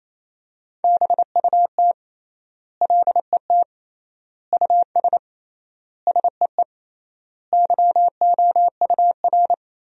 Morse code, heard as a keyed tone, about 28 wpm, 700 Hz. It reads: BUT LET US SEE YOUR
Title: 3x Word-Spacing - Sentences from Top 500 Words - 28wpm